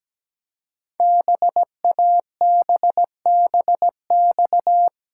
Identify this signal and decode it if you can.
Morse code, heard as a keyed tone, about 17 wpm, 700 Hz. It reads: BABBX
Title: Code Groups - 5 Letters - 17wpm